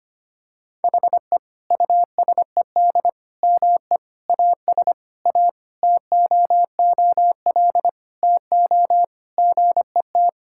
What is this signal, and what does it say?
Morse code, heard as a keyed tone, about 25 wpm, 700 Hz. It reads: HE USED ME AS A TOOL TO GET
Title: Sentences from Top 1000 Words - Review Part 6 - 25wpm